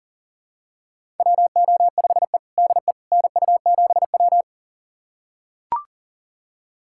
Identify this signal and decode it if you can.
Morse code, heard as a keyed tone, about 40 wpm, 700 Hz. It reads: WO5E DE NU7W
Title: USA Callsign DE Callsign - 4 Characters - 40wpm